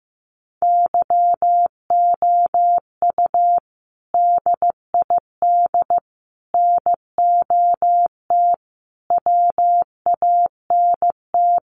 Morse code, heard as a keyed tone, about 15 wpm, 700 Hz. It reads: YOU DID NOT WANT
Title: Sets of 5 Words Encore2 15wpm